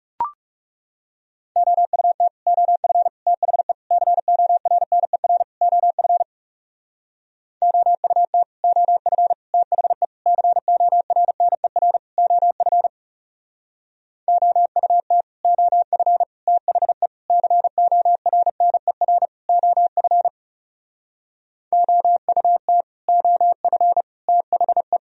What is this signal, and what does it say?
Morse code, heard as a keyed tone, about 38 wpm, 700 Hz. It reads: OUT OF THE CORNER OF OUT OF THE CORNER OF OUT OF THE CORNER OF OUT OF THE